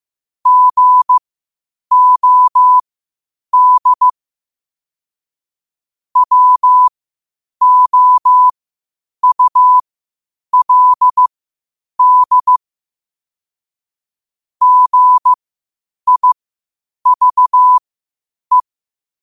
Morse code, heard as a keyed tone, about 15 wpm, spaced as for 8 wpm, 1000 Hz. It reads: GOD WOULD GIVE
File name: qotdfiles / 8wpm